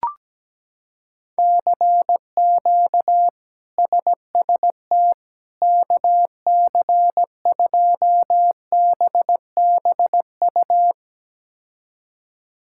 Morse code, heard as a keyed tone, about 17 wpm, 700 Hz. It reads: CQ SST KC2BBU
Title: K1USN SST QSO Practice - 17wpm